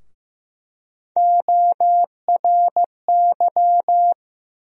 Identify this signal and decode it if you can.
Morse code, heard as a keyed tone, about 15 wpm, 700 Hz. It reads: ORY
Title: USA Call Sign Postfixes - 3 Letters 15wpm